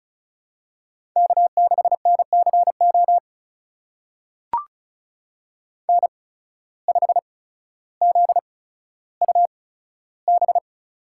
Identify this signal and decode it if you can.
Morse code, heard as a keyed tone, about 36 wpm, spaced as for 15 wpm, 700 Hz. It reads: K6NCO N5ZUB
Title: USA Call Signs - 5 Characters - 8x Character Spacing - 35wpm